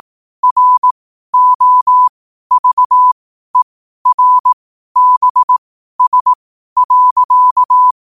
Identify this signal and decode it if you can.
Morse code, heard as a keyed tone, about 18 wpm, spaced as for 12 wpm, 1000 Hz. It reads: ROVERBS.
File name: qotdfiles / 12wpm